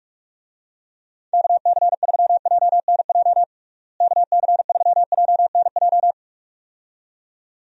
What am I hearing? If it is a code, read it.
Morse code, heard as a keyed tone, about 46 wpm, 700 Hz. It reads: KC3JNJ KC3JNJ
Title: USA Call Signs - 6 Characters - Repeated 2x - 45wpm